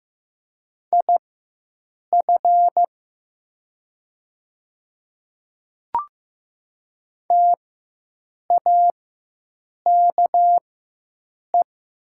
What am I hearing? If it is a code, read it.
Morse code, heard as a keyed tone, about 15 wpm, spaced as for 7 wpm, 700 Hz. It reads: IF TAKE
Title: 4x Character Spacing - Top 300 Words - 15wpm